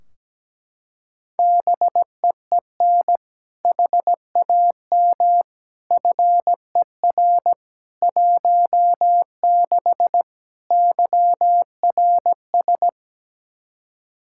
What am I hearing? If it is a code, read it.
Morse code, heard as a keyed tone, about 17 wpm, 700 Hz. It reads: BEEN HAM FER 16 YRS
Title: K1USN SST QSO Practice 17wpm